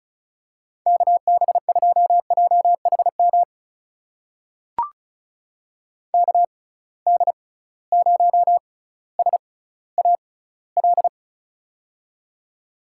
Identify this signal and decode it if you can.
Morse code, heard as a keyed tone, about 36 wpm, spaced as for 18 wpm, 700 Hz. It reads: KB2JHM KD0SAL
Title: USA Call Signs - 6 Characters - 6x Character Spacing - 35wpm